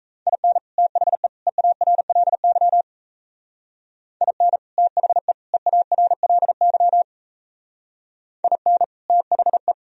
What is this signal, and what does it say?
Morse code, heard as a keyed tone, about 32 wpm, 700 Hz. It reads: IN THE EARLY IN THE EARLY IN THE
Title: Speed-Racing - Sets of 3 Words 28wpm